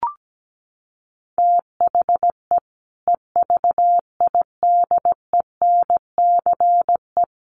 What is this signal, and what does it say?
Morse code, heard as a keyed tone, about 17 wpm, 700 Hz. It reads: THE EVIDENCE
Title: Sentences from Top 1300 Words 17wpm